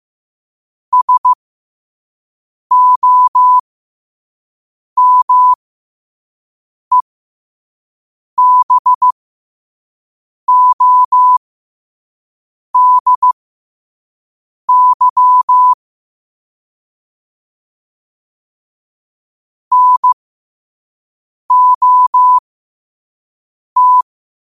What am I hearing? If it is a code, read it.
Morse code, heard as a keyed tone, about 15 wpm, spaced as for 5 wpm, 1000 Hz. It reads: SOMEBODY NOT
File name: qotdfiles / 5wpm